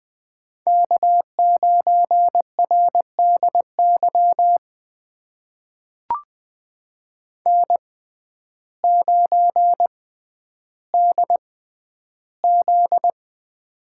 Morse code, heard as a keyed tone, about 20 wpm, spaced as for 11 wpm, 700 Hz. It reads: K9RDY N9DZ
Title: USA Call Signs - 6x Character Spacing - 20wpm